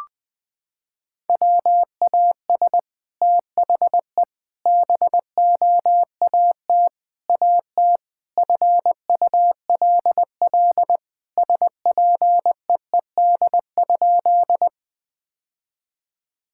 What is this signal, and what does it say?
Morse code, heard as a keyed tone, about 21 wpm, 700 Hz. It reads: WAS THE BOAT AT FULL SPEED?
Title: Sentences from Top 1000 Words - Review Part 5 - 20wpm